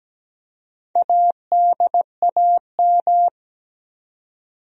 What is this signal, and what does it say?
Morse code, heard as a keyed tone, about 17 wpm, 700 Hz. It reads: ADAM
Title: USA Common Names - Max 4 Letters - 17wpm